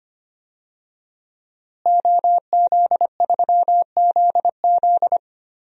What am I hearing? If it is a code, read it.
Morse code, heard as a keyed tone, about 25 wpm, 700 Hz. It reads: OZ3ZZ
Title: DXCC-100 Call Signs - 5 Characters - Repeated 3x - 25wpm